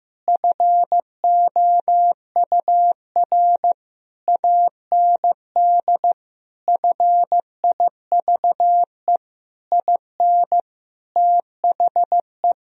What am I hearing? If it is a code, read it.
Morse code, heard as a keyed tone, about 15 wpm, 700 Hz. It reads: FOUR AND FIVE IN THE
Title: Sentences from Top 1000 Words - Review Part 2 - 15wpm